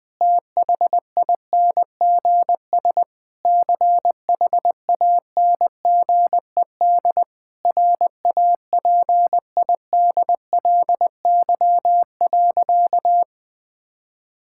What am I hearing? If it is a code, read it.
Morse code, heard as a keyed tone, about 20 wpm, 700 Hz. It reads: THINGS CHANGED RAPIDLY.
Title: Sentences with 5-Word Phrases - 20wpm